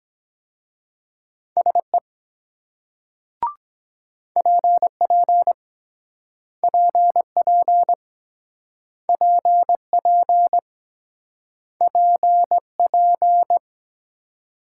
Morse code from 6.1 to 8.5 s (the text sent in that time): PP